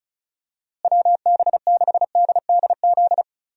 Morse code, heard as a keyed tone, about 34 wpm, 700 Hz. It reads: WB6DDZ
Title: USA Call Signs - 6 Characters - Repeated 3x - 35wpm